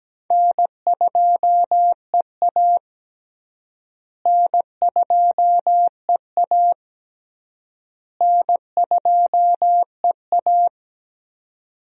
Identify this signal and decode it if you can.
Morse code, heard as a keyed tone, about 17 wpm, 700 Hz. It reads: N2EA N2EA N2EA
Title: USA Call Signs - 4 Characters - Repeated 3x - 17wpm